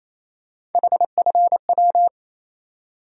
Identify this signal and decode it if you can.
Morse code, heard as a keyed tone, about 28 wpm, 700 Hz. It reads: HFW
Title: USA Call Sign Postfixes - 3 Letters 28wpm